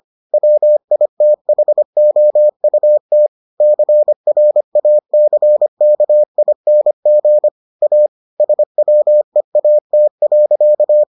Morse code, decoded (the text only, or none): WITHOUT CRACKING A SWEAT.